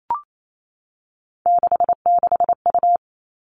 Morse code, heard as a keyed tone, about 28 wpm, 700 Hz. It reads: 66U
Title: Three Challenging Letter-Number Combinations 28wpm